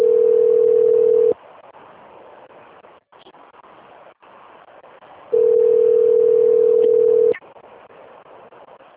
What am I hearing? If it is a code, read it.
Ringback tone (North America), 440 + 480 Hz.